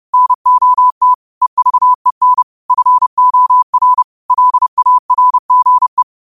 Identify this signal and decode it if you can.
Morse code, heard as a keyed tone, about 30 wpm, 1000 Hz. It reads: NOT EVEN FOR LARGE